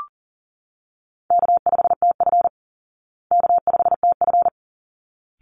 Morse code, heard as a keyed tone, about 40 wpm, 700 Hz. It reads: K5TF K5TF
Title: Easy Call Signs Repeated 2x - 40wpm